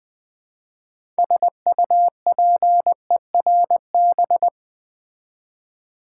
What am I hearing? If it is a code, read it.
Morse code, heard as a keyed tone, about 20 wpm, 700 Hz. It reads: SUPERB
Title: Build A Word - 2 Letters At A Time - 6-Letter Words - 20wpm